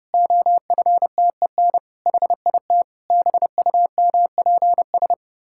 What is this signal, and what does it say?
Morse code, heard as a keyed tone, about 30 wpm, 700 Hz. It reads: OFTEN HIT BUMPS
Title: Pangram Challenge v2 - 30wpm